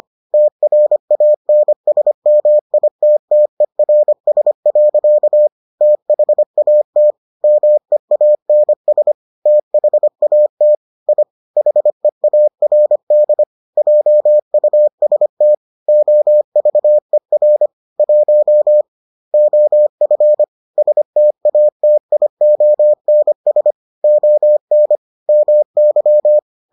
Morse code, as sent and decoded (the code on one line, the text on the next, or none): - .-. .- -. ... -- .. - - . .-. ... .-.-.- / - .... .- - / -- . .- -. ... / - .... .- - / .. / .... .- .-. -.. / .--- ..- ... - / --- ...- . .-. / .---- / --- ..-. / ... - .- - .. --- -. ... / --- -. / -- -.--
TRANSMITTERS. THAT MEANS THAT I HEARD JUST OVER 1 OF STATIONS ON MY